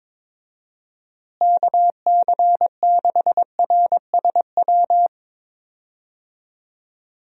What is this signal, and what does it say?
Morse code, heard as a keyed tone, about 22 wpm, 700 Hz. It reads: KC6RSW
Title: USA Call Signs - 6 Characters - Repeated 3x - 22wpm